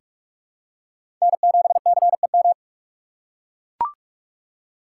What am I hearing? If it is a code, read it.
Morse code, heard as a keyed tone, about 44 wpm, 700 Hz. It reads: N7CEM
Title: USA Call Signs - 6x Character Spacing - 45wpm